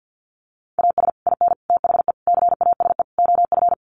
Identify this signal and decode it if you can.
Morse code, heard as a keyed tone, about 50 wpm, 700 Hz. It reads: AS IN THE CASE OF
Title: Sets of 5 Words 50wpm